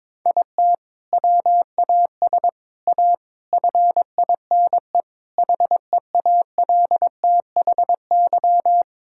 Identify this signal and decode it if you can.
Morse code, heard as a keyed tone, about 21 wpm, 700 Hz. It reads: IT WAS A FINE HEALTHY
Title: Sentences from Top 1000 Words - Review Part 5 - 22wpm